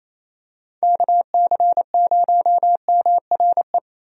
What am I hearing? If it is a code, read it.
Morse code, heard as a keyed tone, about 28 wpm, 700 Hz. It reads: KC0MRE
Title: USA Call Signs - 6 Characters - Repeated 2x - 28wpm